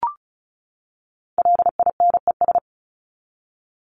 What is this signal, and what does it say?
Morse code, heard as a keyed tone, about 34 wpm, 700 Hz. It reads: LINES
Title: ES:S Suffix - Top 100 Words with Base Words 35wpm